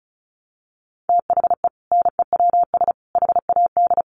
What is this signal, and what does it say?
Morse code, heard as a keyed tone, about 34 wpm, 700 Hz. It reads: THE NEWS HAD
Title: Sentences from Top 1200 Words 35wpm